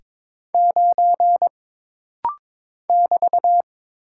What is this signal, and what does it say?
Morse code, heard as a keyed tone, about 22 wpm, 700 Hz. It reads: 9 <BT>
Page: This rapid-fire timing is geared to push your ICR (Instant Character Recognition) skills.